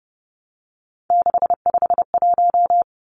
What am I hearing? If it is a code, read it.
Morse code, heard as a keyed tone, about 30 wpm, 700 Hz. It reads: 651
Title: Numbers - 3-Digit 30wpm